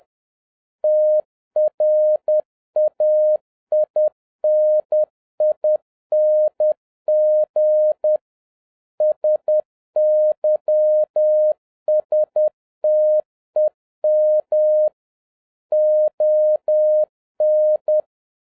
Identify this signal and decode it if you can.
Morse code, heard as a keyed tone, about 10 wpm, 610 Hz. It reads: TRAINING SYSTEM ON